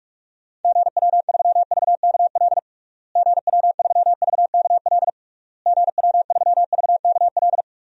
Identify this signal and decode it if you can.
Morse code, heard as a keyed tone, about 46 wpm, 700 Hz. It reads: GW3VKL GW3VKL GW3VKL
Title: DXCC-100 Call Signs - 6 Characters - Repeated 3x - 45wpm